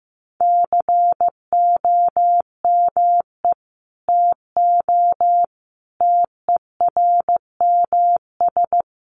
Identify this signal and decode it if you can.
Morse code, heard as a keyed tone, about 15 wpm, 700 Hz. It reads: COME TO TERMS
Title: Sets of 5 Words Encore2 15wpm